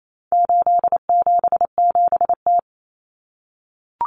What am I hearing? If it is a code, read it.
Morse code, heard as a keyed tone, about 28 wpm, 700 Hz. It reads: 877T
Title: Common Cut Numbers - 4-Digit 28wpm